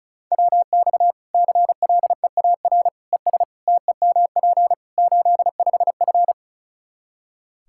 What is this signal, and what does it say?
Morse code, heard as a keyed tone, about 34 wpm, 700 Hz. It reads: WX CLEAR ES TEMP 85F
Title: Practice QSO - Exchange 2 - Info - 35wpm